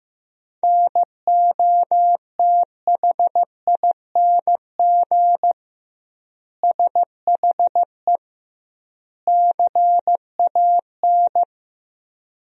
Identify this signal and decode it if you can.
Morse code, heard as a keyed tone, about 15 wpm, 700 Hz. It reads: NOTHING SHE CAN
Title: 2x Word-Spacing - Sentences from Top 500 Words - 15wpm